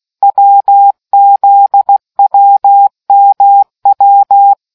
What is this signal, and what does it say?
Morse code, heard as a keyed tone, about 16 wpm, 800 Hz. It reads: WZWMW